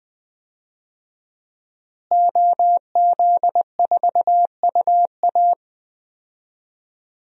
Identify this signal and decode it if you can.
Morse code, heard as a keyed tone, about 20 wpm, 700 Hz. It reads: OZ4UA
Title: DXCC-100 Call Signs - 5 Characters - Repeated 2x - 20wpm